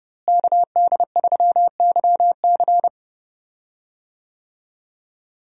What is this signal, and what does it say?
Morse code, heard as a keyed tone, about 30 wpm, 700 Hz. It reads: KD3YC
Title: USA Call Signs - 5 Characters - Repeated 3x - 30wpm